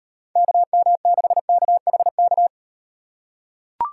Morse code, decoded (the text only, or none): KM6KHK